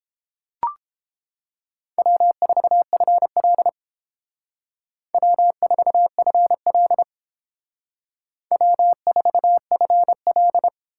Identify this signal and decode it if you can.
Morse code, heard as a keyed tone, about 30 wpm, 700 Hz. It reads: W4FL W4FL W4FL